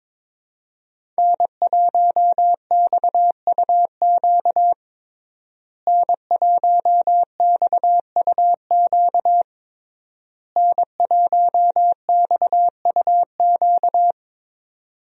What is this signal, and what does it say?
Morse code, heard as a keyed tone, about 22 wpm, 700 Hz. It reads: N1XUQ N1XUQ N1XUQ
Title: USA Call Signs - 5 Characters - Repeated 3x - 22wpm